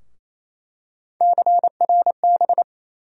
Morse code, heard as a keyed tone, about 28 wpm, 700 Hz. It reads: CRB
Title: USA Call Sign Postfixes - 3 Letters 28wpm